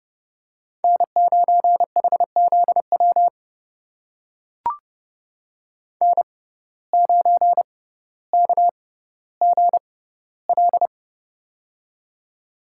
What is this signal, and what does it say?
Morse code, heard as a keyed tone, about 30 wpm, spaced as for 15 wpm, 700 Hz. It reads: N9HZW N9KGL
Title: USA Call Signs - 5 Characters - 6x Character Spacing - 30wpm